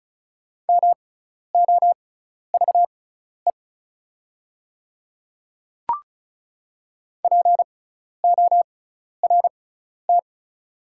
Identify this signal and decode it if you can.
Morse code, heard as a keyed tone, about 36 wpm, spaced as for 12 wpm, 700 Hz. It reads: MOVE PORT
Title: 6x Character Spacing - Top 200 Words - 35wpm